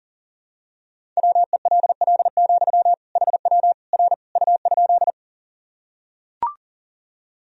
Morse code, heard as a keyed tone, about 37 wpm, 700 Hz. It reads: WELL, HW R U?
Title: Sentences from Top 200 Words and Abbreviations 40wpm